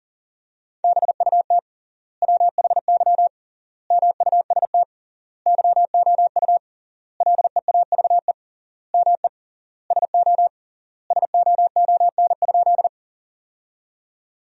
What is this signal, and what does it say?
Morse code, heard as a keyed tone, about 40 wpm, 700 Hz. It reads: BUT WHY MUST YOU LEAVE ME SO SOON?
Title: 3x Word-Spacing - Sentences from Top 400 Words - 40wpm